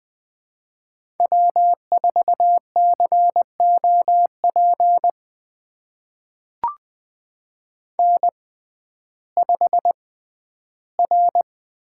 Morse code, heard as a keyed tone, about 21 wpm, spaced as for 12 wpm, 700 Hz. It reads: W4COP N5R
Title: USA Call Signs - 5 Characters - 6x Character Spacing - 20wpm